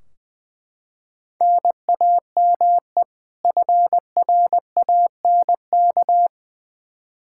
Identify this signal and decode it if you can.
Morse code, heard as a keyed tone, about 20 wpm, 700 Hz. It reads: NAME FRANK
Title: Practice QSO - Exchange 1 - Beginning - 20wpm